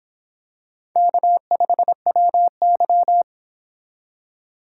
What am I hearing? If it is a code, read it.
Morse code, heard as a keyed tone, about 26 wpm, 700 Hz. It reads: K5WY